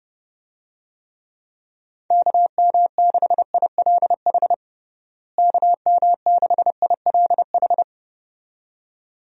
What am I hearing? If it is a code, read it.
Morse code, heard as a keyed tone, about 31 wpm, 700 Hz. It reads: KM6ILH KM6ILH
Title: USA Call Signs - 6 Characters - Repeated 2x - 30wpm